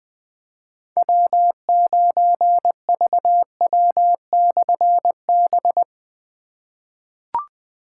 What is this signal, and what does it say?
Morse code, heard as a keyed tone, about 20 wpm, 700 Hz. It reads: W9VW/B
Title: Difficult Call Signs 20wpm